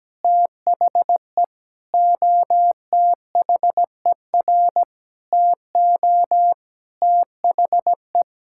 Morse code, heard as a keyed tone, about 17 wpm, 700 Hz. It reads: THE OTHER TO THE
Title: Sentences from Top 1000 Words - Review Part 6 - 17wpm